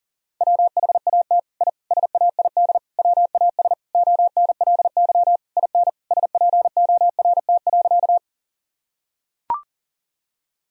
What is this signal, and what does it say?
Morse code, heard as a keyed tone, about 40 wpm, 700 Hz. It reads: WHAT I SAID WAS ONLY IN SPORT.
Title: Sentences from Top 1000 Words - Review Part 4 - 40wpm